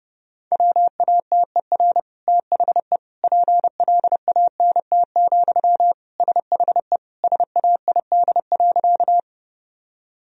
Morse code, heard as a keyed tone, about 30 wpm, 700 Hz. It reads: WATER THE PLANT, SHE SAID.
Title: Sentences from Top 1000 Words - Review Part 1 - 30wpm